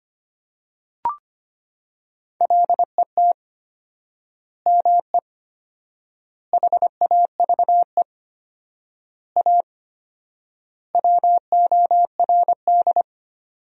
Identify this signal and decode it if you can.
Morse code, heard as a keyed tone, about 25 wpm, 700 Hz. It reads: LET ME HAVE A WORD
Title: 4x Word-Spacing - Sentences from Top 300 Words - 25wpm